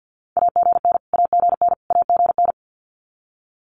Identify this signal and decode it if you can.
Morse code, heard as a keyed tone, about 50 wpm, 700 Hz. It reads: AGN AGN AGN
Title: Abbreviations Repeated 3x - 50wpm